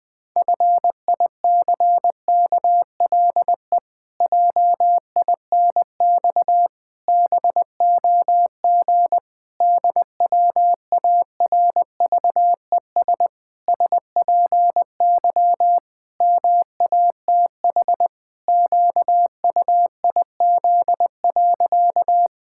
Morse code, heard as a keyed tone, about 20 wpm, 700 Hz. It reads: FICKLE JINX BOG DWARVES SPY MATH QUIZ.
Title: Pangram Challenge v2 - 20wpm